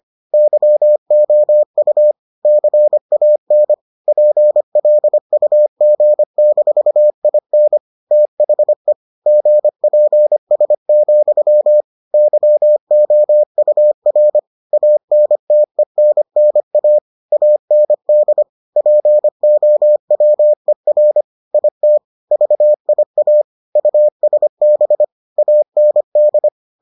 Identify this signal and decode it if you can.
Morse code, heard as a keyed tone, about 25 wpm, 600 Hz. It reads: YOU CAN PLUG-IN THE GPS, YOUR ANTENNA AND POWER IT VIA USB AND